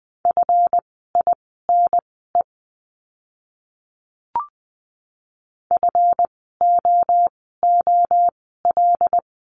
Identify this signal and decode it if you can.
Morse code, heard as a keyed tone, about 20 wpm, spaced as for 15 wpm, 700 Hz. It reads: FINE FOOL
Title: 139 Lesson 23f - Letter f - Words 20wpm